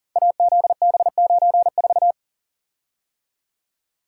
Morse code, heard as a keyed tone, about 40 wpm, 700 Hz. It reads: AZB94
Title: Code Groups - 5 Letter:Numbers - 40wpm